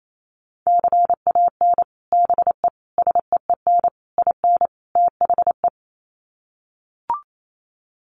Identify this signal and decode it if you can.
Morse code, heard as a keyed tone, about 28 wpm, 700 Hz. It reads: CAN BE SEEN IN THE
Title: Sets of 5 Words Encore2 28wpm